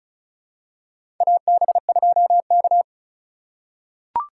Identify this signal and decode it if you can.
Morse code, heard as a keyed tone, about 34 wpm, 700 Hz. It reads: AB2K
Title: USA Call Signs - 4 Characters - 4x Character Spacing - 35wpm